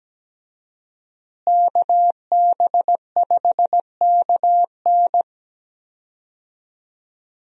Morse code, heard as a keyed tone, about 17 wpm, 700 Hz. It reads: KB5KN
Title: USA Call Signs - 5 Characters - Repeated 2x - 17wpm